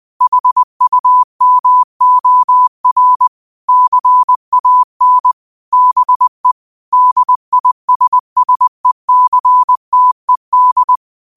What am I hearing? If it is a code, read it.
Morse code, heard as a keyed tone, about 20 wpm, 1000 Hz. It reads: HUMOR CAN BE DISSECTED
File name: qotdfiles / 20wpm